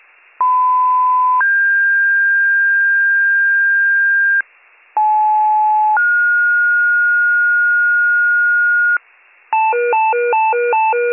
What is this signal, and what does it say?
Two-tone page, heard as a one-second tone, then a three-second tone, about 1010 then 1600 Hz, about 860 then 1430 Hz, with noise. It is heard from 0.4 to 9.0 s.